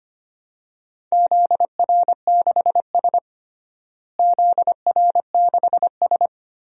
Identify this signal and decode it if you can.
Morse code, heard as a keyed tone, about 25 wpm, 700 Hz. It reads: ZR6S ZR6S
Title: DXCC-100 Call Signs - 4 Characters - Repeated 3x - 25wpm